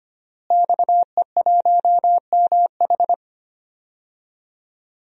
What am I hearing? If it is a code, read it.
Morse code, heard as a keyed tone, about 25 wpm, 700 Hz. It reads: XE1MH
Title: DXCC-100 Call Signs - 5 Characters - Repeated 3x - 25wpm